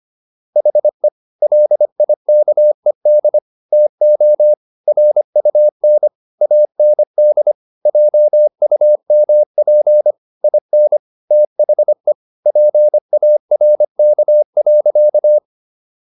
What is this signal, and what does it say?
Morse code, heard as a keyed tone, about 25 wpm, 600 Hz. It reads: HE LIKED TO RUN AND JUMP IN THE PARK.